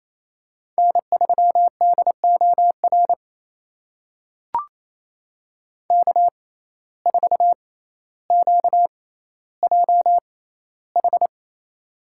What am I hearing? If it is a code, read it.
Morse code, heard as a keyed tone, about 28 wpm, spaced as for 14 wpm, 700 Hz. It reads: N3DOR K4QJH
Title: USA Call Signs - 5 Characters - 6x Character Spacing - 28wpm